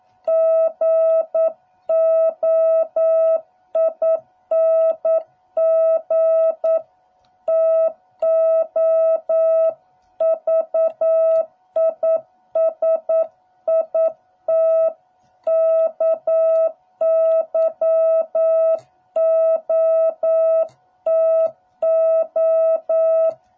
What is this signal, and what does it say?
Morse code, heard as a keyed tone, about 9 wpm, 650 Hz. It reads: GOING TO VISIT KYOTO